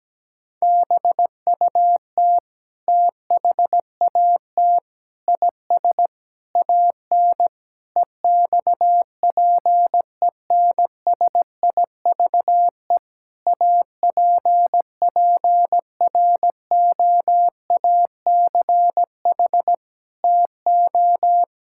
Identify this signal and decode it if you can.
Morse code, heard as a keyed tone, about 17 wpm, 700 Hz. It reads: BUT THAT IS AN EXPENSIVE APPROACH TO
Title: Sentences from Top 1000 Words - Review Part 6 - 17wpm